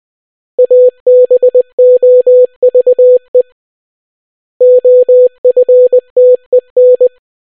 Morse code, heard as a keyed tone, about 20 wpm, 500 Hz. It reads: ABOVE OFTEN